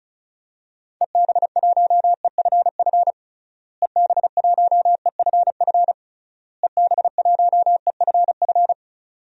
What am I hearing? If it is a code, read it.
Morse code, heard as a keyed tone, about 36 wpm, 700 Hz. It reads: EB1EFF EB1EFF EB1EFF
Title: DXCC-100 Call Signs - 6 Characters - Repeated 3x - 35wpm